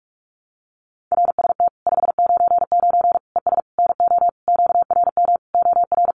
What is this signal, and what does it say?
Morse code, heard as a keyed tone, about 44 wpm, 700 Hz. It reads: RST 599 ES NO QRM OR
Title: Common QSO Phrases 45wpm